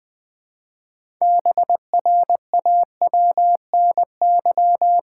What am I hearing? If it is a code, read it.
Morse code, heard as a keyed tone, about 21 wpm, 700 Hz. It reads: BRAWNY
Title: Pangram Challenge v2 - 20wpm